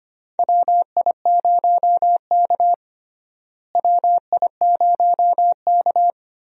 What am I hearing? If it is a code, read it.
Morse code, heard as a keyed tone, about 25 wpm, 700 Hz. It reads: WI0K WI0K
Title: USA Call Signs - 4 Characters - Repeated 2x - 25wpm